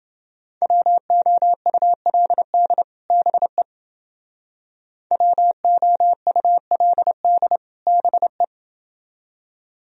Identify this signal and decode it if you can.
Morse code, heard as a keyed tone, about 29 wpm, 700 Hz. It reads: WOULD BE WOULD BE